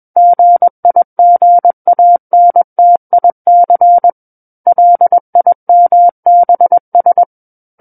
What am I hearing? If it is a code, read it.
Morse code, heard as a keyed tone, about 21 wpm, 700 Hz. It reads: GIGANTIC LIMBS
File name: Book One / 12-Chapter 12 - 21 WPM